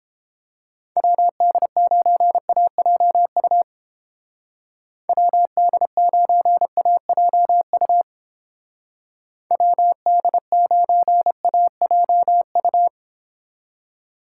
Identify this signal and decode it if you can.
Morse code, heard as a keyed tone, about 29 wpm, 700 Hz. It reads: WD9AJU WD9AJU WD9AJU